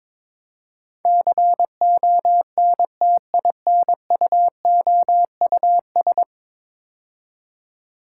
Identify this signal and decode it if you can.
Morse code, heard as a keyed tone, about 22 wpm, 700 Hz. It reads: CONTINUOUS
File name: practice / 22wpm / 10 Letter Words - First 3 Letters Given - 22wpm